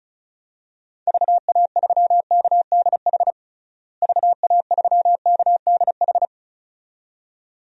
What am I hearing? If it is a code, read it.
Morse code, heard as a keyed tone, about 36 wpm, 700 Hz. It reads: VA3KDH VA3KDH
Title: DXCC-100 Call Signs - 6 Characters - Repeated 2x - 35wpm